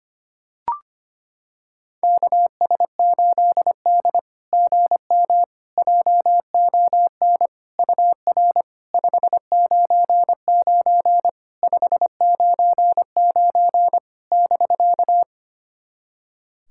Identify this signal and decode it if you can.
Morse code, heard as a keyed tone, about 25 wpm, 700 Hz. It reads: KS8D GM JON UR 599 599 <BK>
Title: SOTA QSO Practice 25wpm